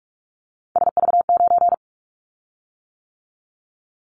Morse code, heard as a keyed tone, about 44 wpm, 700 Hz. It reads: SV9
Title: Callsign Prefixes for Top 100 DXCC - 45wpm